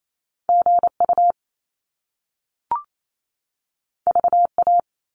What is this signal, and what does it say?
Morse code, heard as a keyed tone, about 28 wpm, 700 Hz. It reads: G U V A
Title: Top Two Letter Combinations in Words 28wpm